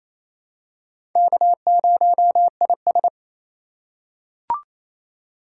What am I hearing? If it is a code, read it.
Morse code, heard as a keyed tone, about 28 wpm, 700 Hz. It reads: K0IS
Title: USA Call Signs - 4 Characters - 2x Character Spacing - 28wpm